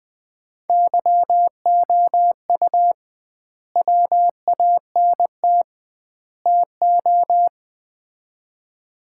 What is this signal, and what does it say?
Morse code, heard as a keyed tone, about 20 wpm, 700 Hz. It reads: YOU WANT TO
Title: 2x Word-Spacing - Sets of 3 Words - 20wpm